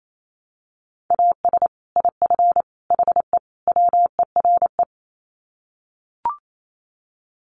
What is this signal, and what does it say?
Morse code, heard as a keyed tone, about 28 wpm, 700 Hz. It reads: AS IF HE WERE
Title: Sets of 4 Words 28wpm